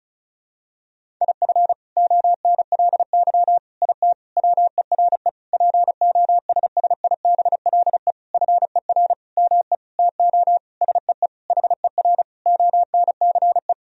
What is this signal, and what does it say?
Morse code, heard as a keyed tone, about 36 wpm, 700 Hz. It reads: IF ONLY IT WERE POSSIBLE FER ME TO SEE HER ONCE
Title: Sentences from Top 500 Words and Abbreviations 35wpm